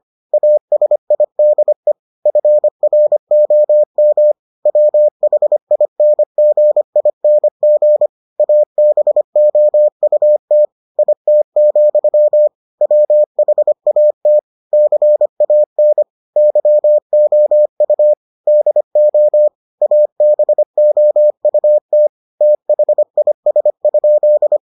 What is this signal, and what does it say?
Morse code, heard as a keyed tone, about 25 wpm, 600 Hz. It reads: ASIDE FROM WHINGING ABOUT IT, WHAT CAN YOU DO ABOUT THIS?